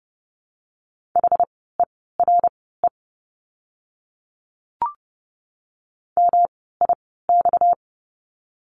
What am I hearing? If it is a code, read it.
Morse code, heard as a keyed tone, about 30 wpm, spaced as for 17 wpm, 700 Hz. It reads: HERE MIX
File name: course / 30:10wpm / 230 Lesson 37f - Letter x - Words 30wpm